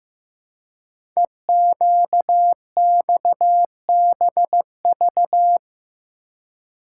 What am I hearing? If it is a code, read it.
Morse code, heard as a keyed tone, about 15 wpm, 700 Hz. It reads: EQXBV
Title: Code Groups - 5 Letter:Numbers - 15wpm